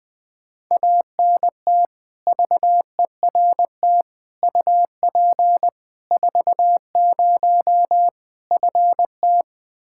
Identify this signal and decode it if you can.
Morse code, heard as a keyed tone, about 20 wpm, 700 Hz. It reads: ANT VERT UP 40 FT
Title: Practice QSO - Exchange 2 - Info - 20wpm